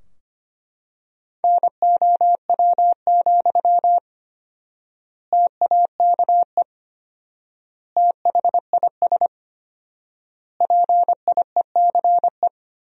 Morse code, heard as a keyed tone, about 25 wpm, 700 Hz. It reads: NOW, TAKE THIS PIECE
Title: 4x Word-Spacing - Sentences from Top 400 Words - 25wpm